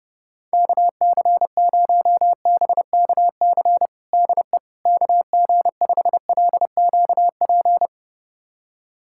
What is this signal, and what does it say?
Morse code, heard as a keyed tone, about 30 wpm, 700 Hz. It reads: KC0BKC DE KG5LQP
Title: USA Callsign DE Callsign - 6 Characters - 30wpm